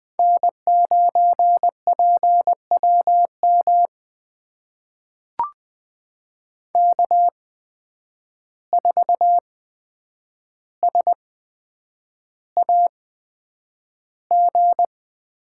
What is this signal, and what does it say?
Morse code, heard as a keyed tone, about 20 wpm, spaced as for 9 wpm, 700 Hz. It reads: N9PWM K4SAG
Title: USA Call Signs - 5 Characters - 8x Character Spacing - 20wpm